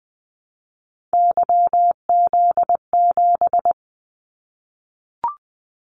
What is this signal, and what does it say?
Morse code, heard as a keyed tone, about 20 wpm, 700 Hz. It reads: YZ7
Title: Callsign Prefixes for Top 25 DXCC - Repeated 3x - 20wpm